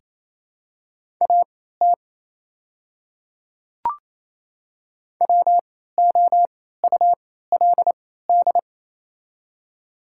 Morse code, heard as a keyed tone, about 29 wpm, spaced as for 16 wpm, 690 Hz. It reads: AT WOULD
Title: 3x Character Spacing - Top 100 Words - 28wpm